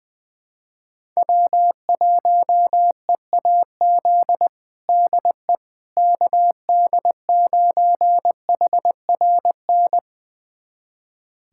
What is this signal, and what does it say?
Morse code, heard as a keyed tone, about 20 wpm, 700 Hz. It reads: W1EAZ DE KD9HRN
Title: USA Callsign DE Callsign 20wpm